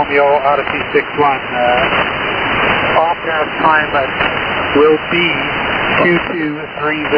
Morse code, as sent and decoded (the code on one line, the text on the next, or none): none